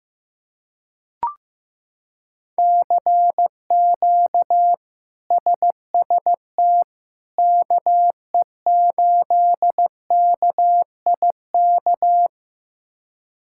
Morse code, heard as a keyed tone, about 15 wpm, 700 Hz. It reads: CQ SST KE8KIK
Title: K1USN SST QSO Practice - 15wpm